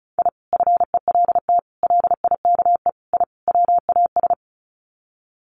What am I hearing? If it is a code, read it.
Morse code, heard as a keyed tone, about 34 wpm, 700 Hz. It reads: I FELT LIKE I WAS
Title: Sets of 5 Words Encore 35wpm